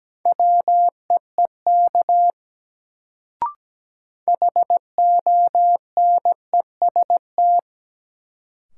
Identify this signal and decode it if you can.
Morse code, heard as a keyed tone, about 17 wpm, 700 Hz. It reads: WEEK HONEST
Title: Sets of 5 Words Encore - New Word Drill - 17wpm